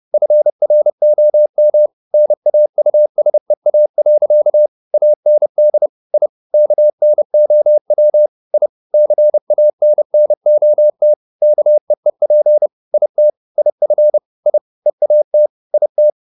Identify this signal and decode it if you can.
Morse code, heard as a keyed tone, about 30 wpm, 600 Hz. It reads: FROM NAUSEA. AND I KNOW I CANNOT KEEP IT IF I EAT IT